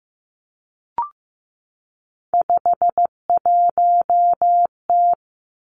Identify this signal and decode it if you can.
Morse code, heard as a keyed tone, about 15 wpm, 700 Hz. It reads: 51T
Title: Three Challenging Letter-Number Combinations 15wpm